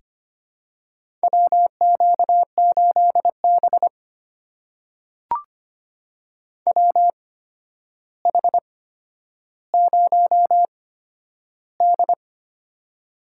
Morse code, heard as a keyed tone, about 25 wpm, spaced as for 11 wpm, 700 Hz. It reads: WQ8B WH0D
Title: USA Call Signs - 4 Characters - 8x Character Spacing - 25wpm